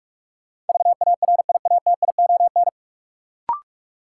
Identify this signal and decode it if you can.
Morse code, heard as a keyed tone, about 43 wpm, 700 Hz. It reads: VARIATION
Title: Sentences from Top 1600 Words - New Word Drill Alt - 45wpm